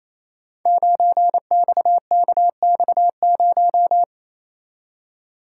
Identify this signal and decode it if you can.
Morse code, heard as a keyed tone, about 29 wpm, 700 Hz. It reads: 9XKX0
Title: Code Groups - 5 Letter:Numbers - 28wpm